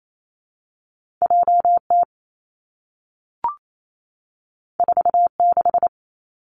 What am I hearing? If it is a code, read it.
Morse code, heard as a keyed tone, about 28 wpm, 700 Hz. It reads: J T 4 6
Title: Two Challenging Letter-Number Combinations 28wpm